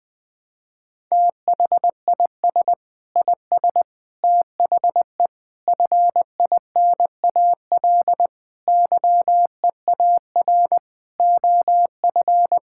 Morse code, heard as a keyed tone, about 21 wpm, 700 Hz. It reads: THIS IS THE FINAL YEAR OF
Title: Sentences with 6-Word Phrases - Spoken Beginning - 20wpm